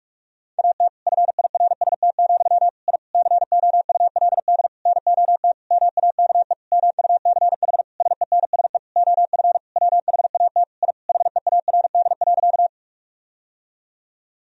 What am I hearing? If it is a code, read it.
Morse code, heard as a keyed tone, about 46 wpm, 700 Hz. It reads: AT FIRST, I COULD NOT MAKE MUCH SENSE OF WHAT I HEARD.
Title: Sentences from Top 1000 Words - Review Part 4 - 45wpm